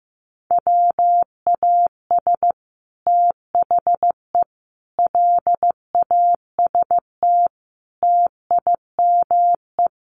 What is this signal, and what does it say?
Morse code, heard as a keyed tone, about 15 wpm, 700 Hz. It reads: WAS THE LAST TIME
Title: Sets of 5 Words Encore2 15wpm